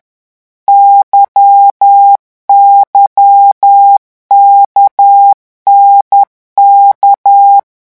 Morse code, heard as a keyed tone, about 10 wpm, 800 Hz. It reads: YYKNK
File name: morse mp3 / revision3-vitesse 10 mots